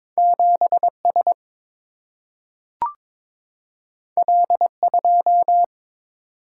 Morse code, heard as a keyed tone, about 22 wpm, 700 Hz. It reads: 7S L2
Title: Single-Letter Single-Number 22wpm